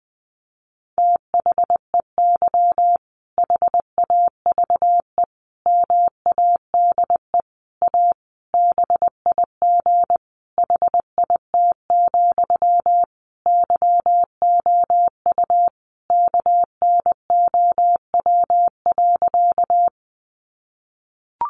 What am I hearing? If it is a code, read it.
Morse code, heard as a keyed tone, about 20 wpm, 700 Hz. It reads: THEY HAVE MADE A BIG HIT, YOU KNOW.
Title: Sentences from Top 1200 Words 20wpm